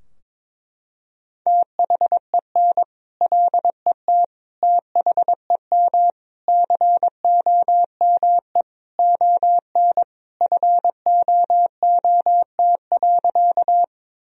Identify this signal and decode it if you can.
Morse code, heard as a keyed tone, about 22 wpm, 700 Hz. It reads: THEN LET THEM COME ON FOOT.
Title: Sentences from Top 1000 Words - Review Part 3 - 22wpm